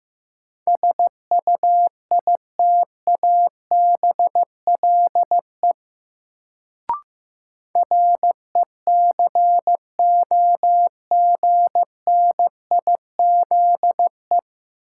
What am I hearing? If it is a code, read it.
Morse code, heard as a keyed tone, about 15 wpm, 700 Hz. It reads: SUITABLE RECOGNIZE
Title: Sentences from Top 1400 Words - New Word Drill - 15wpm